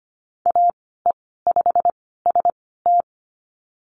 Morse code, heard as a keyed tone, about 25 wpm, spaced as for 16 wpm, 700 Hz. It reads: AE5ST
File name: Lesson 15g - Number 5 - Call Signs 25wpm